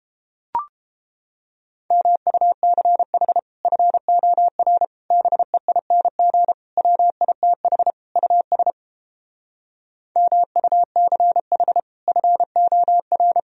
Morse code, heard as a keyed tone, about 31 wpm, 700 Hz. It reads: MUCH FOR BEING WITH US MUCH FOR